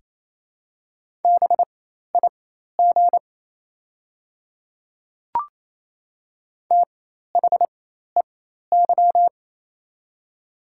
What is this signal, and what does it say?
Morse code, heard as a keyed tone, about 28 wpm, spaced as for 13 wpm, 700 Hz. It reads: BIG THEY
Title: 4x Character Spacing - Top 200 Words - 28wpm